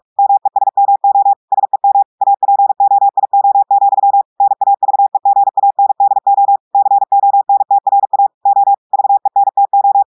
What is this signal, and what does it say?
Morse code, heard as a keyed tone, about 46 wpm, 800 Hz. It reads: MESMO SEM APOIO, NAVEGANDO CONTRA O VENTO
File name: portugais texte en clair 1 vitesse 45 mots minutes